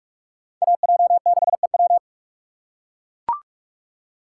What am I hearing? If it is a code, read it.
Morse code, heard as a keyed tone, about 42 wpm, 700 Hz. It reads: AJ6EW